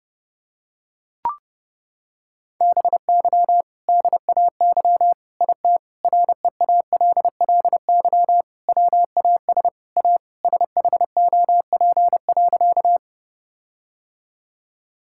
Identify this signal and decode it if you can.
Morse code, heard as a keyed tone, about 30 wpm, 700 Hz. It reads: BY DAY IT REALLY WAS A SHOP.
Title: Sentences from Top 1600 Words 30wpm